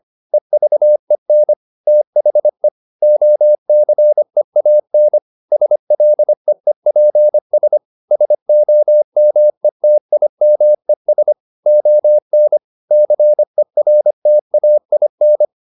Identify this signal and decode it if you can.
Morse code, heard as a keyed tone, about 25 wpm, 600 Hz. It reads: EVEN THE OCEAN SLEEPS SOMETIMES ON CERTAIN